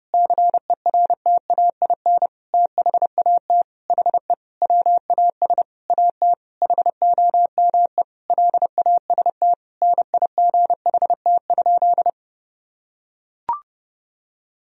Morse code, heard as a keyed tone, about 30 wpm, 700 Hz. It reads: CERTAIN THAT HE WAS AT HOME LAST NIGHT?
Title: Sentences from Top 1000 Words - Review Part 2 - 30wpm